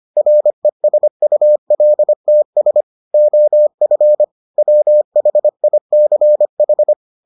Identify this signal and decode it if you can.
Morse code, heard as a keyed tone, about 25 wpm, 600 Hz. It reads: RESULTS OF WHICH